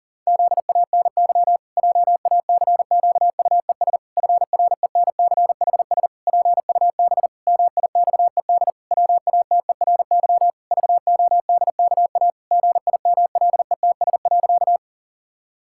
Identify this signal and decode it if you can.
Morse code, heard as a keyed tone, about 40 wpm, 700 Hz. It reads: ZANY JACQUES FRENCHS PUB MIXED WATERY VODKA GIMLETS.
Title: Pangram Challenge v2 - 40wpm